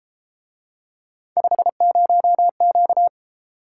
Morse code, heard as a keyed tone, about 32 wpm, 700 Hz. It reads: 50Q